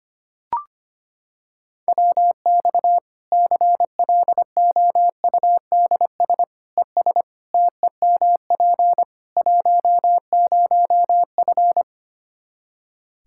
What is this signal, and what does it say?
Morse code, heard as a keyed tone, about 25 wpm, 700 Hz. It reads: WX CLOUDS ES TEMP 10F
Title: Practice QSO - Exchange 2 - Info - 25wpm